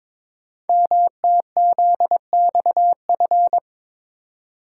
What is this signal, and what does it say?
Morse code, heard as a keyed tone, about 22 wpm, 700 Hz. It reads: MTZXF
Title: Code Groups - 5 Letters - 22wpm